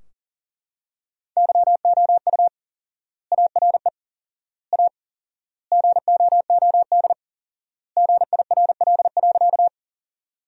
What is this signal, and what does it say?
Morse code, heard as a keyed tone, about 40 wpm, 700 Hz. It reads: YOU ARE A GOOD GIRL.
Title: 4x Word-Spacing - Sentences from Top 400 Words - 40wpm